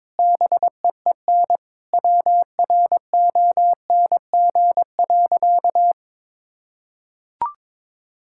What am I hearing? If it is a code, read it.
Morse code, heard as a keyed tone, about 22 wpm, 700 Hz. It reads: BEEN WRONG.
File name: Sentences with 9-Word Phrases - 22wpm